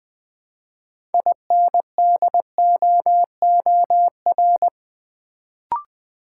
Morse code, heard as a keyed tone, about 20 wpm, 700 Hz. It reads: INDOOR
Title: Build A Word - 3 Letters At A Time - 6-Letter Words - 20wpm